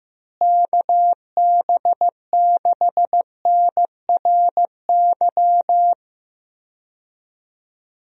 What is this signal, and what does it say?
Morse code, heard as a keyed tone, about 15 wpm, 700 Hz. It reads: KB6NRY
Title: USA Call Signs - 6 Characters - Repeated 3x - 15wpm